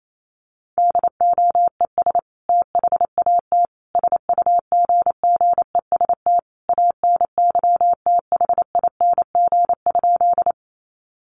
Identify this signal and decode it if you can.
Morse code, heard as a keyed tone, about 28 wpm, 700 Hz. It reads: DOES THAT SUGGEST ANYTHING?
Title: Sentences from Top 1300 Words 28wpm